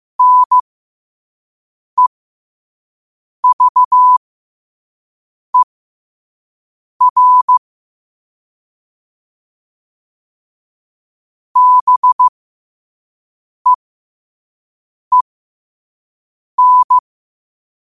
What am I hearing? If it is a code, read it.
Morse code, heard as a keyed tone, about 15 wpm, spaced as for 5 wpm, 1000 Hz. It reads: NEVER BEEN
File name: qotdfiles / 5wpm